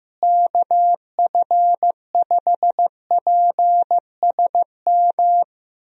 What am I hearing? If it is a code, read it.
Morse code, heard as a keyed tone, about 16 wpm, 700 Hz. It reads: KF5PSM
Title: USA Call Signs - 6 Characters - Repeated 2x - 15wpm